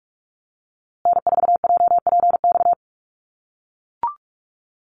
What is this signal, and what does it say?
Morse code, heard as a keyed tone, about 44 wpm, 700 Hz. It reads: N4JPX
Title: USA Call Signs Repeated 3x 45wpm